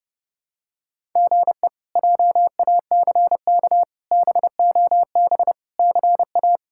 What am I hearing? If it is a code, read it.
Morse code, heard as a keyed tone, about 30 wpm, 700 Hz. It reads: GE JACK BOB CA
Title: K1USN SST QSO Practice - 30wpm